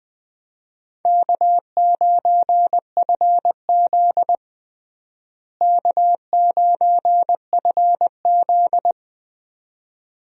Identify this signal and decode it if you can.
Morse code, heard as a keyed tone, about 21 wpm, 700 Hz. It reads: K9FZ K9FZ
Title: USA Call Signs - 4 Characters - Repeated 2x - 20wpm